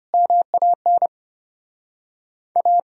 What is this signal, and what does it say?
Morse code, heard as a keyed tone, about 36 wpm, 700 Hz. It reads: MAN A